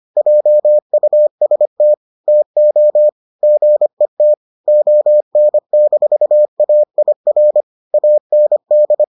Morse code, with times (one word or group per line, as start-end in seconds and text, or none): 0.2-1.9 JUST
2.3-3.1 TO
3.4-4.3 GET
4.7-7.6 ON-AIR
7.9-9.0 AND